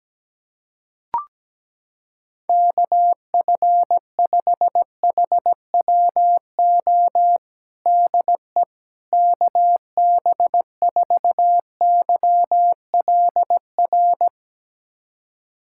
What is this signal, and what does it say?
Morse code, heard as a keyed tone, about 17 wpm, 700 Hz. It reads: KF5HWO DE KB4YLR
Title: USA Callsign DE Callsign - 6 Characters - 17wpm